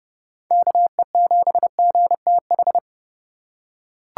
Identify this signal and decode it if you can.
Morse code, heard as a keyed tone, about 30 wpm, 700 Hz. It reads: KE7GTH
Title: USA Call Signs - 6 Characters - 2x Character Spacing - 30wpm